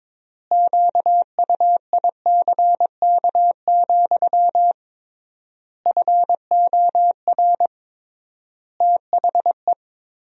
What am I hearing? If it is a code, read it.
Morse code, heard as a keyed tone, about 22 wpm, 700 Hz. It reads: QUICK, FOR THE
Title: 3x Word-Spacing - Sentences from Top 500 Words - 22wpm